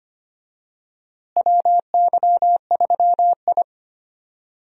Morse code, heard as a keyed tone, about 25 wpm, 700 Hz. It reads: WY3I
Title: USA Call Signs - 4 Characters - Repeated 3x - 25wpm